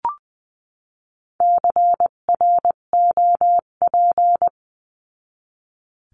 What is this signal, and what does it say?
Morse code, heard as a keyed tone, about 20 wpm, 700 Hz. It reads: CROP
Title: 258 Lesson 42c - Words 16-20wpm